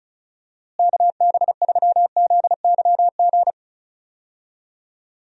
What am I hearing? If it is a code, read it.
Morse code, heard as a keyed tone, about 36 wpm, 700 Hz. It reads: KB3ZYG
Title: USA Call Signs - 6 Characters - Repeated 3x - 35wpm